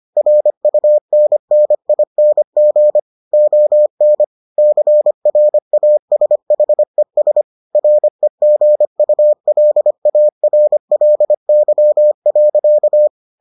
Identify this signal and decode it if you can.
Morse code, heard as a keyed tone, about 25 wpm, 600 Hz. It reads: RUNNING ON CRASHES REGULARLY.